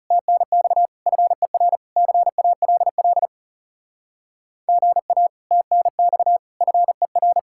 Morse code, heard as a keyed tone, about 38 wpm, 700 Hz. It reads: TNX FER CALL GA TNX FER